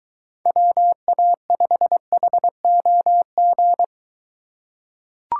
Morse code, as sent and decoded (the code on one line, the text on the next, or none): .-- .- ..... .... --- --.
WA5HOG